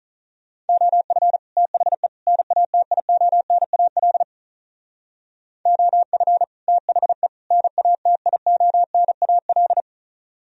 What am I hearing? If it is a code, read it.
Morse code, heard as a keyed tone, about 42 wpm, 700 Hz. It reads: OF THE NATIONAL OF THE NATIONAL
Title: Speed-Racing - Sets of 3 Words 35wpm